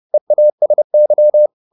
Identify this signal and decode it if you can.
Morse code, heard as a keyed tone, about 30 wpm, 600 Hz. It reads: EASY